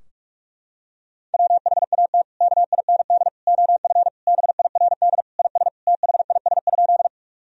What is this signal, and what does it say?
Morse code, heard as a keyed tone, about 46 wpm, 700 Hz. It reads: WHAT KIND OF BIRD IS THIS?
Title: Sentences from Top 1000 Words - Review Part 2 - 45wpm